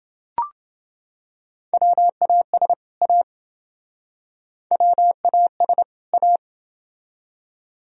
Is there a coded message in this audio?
Yes: Morse code from 1.7 to 6.4 s, heard as a keyed tone, about 28 wpm, 700 Hz. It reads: WAS A WAS A